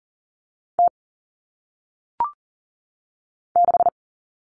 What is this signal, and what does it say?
Morse code, heard as a keyed tone, about 40 wpm, 700 Hz. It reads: T 6
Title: Common Cut Numbers - 1-Digit 40wpm